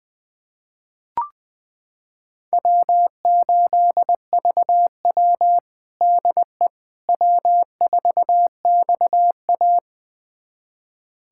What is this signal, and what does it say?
Morse code, heard as a keyed tone, about 21 wpm, 700 Hz. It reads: W8VW DE W4XA
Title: USA Callsign DE Callsign - 4 Characters - 20wpm